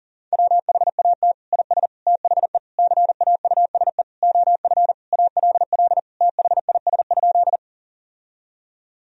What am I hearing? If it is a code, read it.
Morse code, heard as a keyed tone, about 40 wpm, 700 Hz. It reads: WHAT IS THE CAUSE OF ALL THIS?
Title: Sentences from Top 1000 Words - Review Part 1 - 40wpm